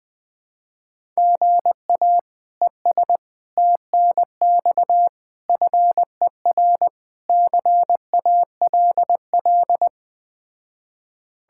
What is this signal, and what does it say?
Morse code, heard as a keyed tone, about 20 wpm, 700 Hz. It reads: GA ES TNX FER CALL
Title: Speed-Racing Common QSO Phrases - 17wpm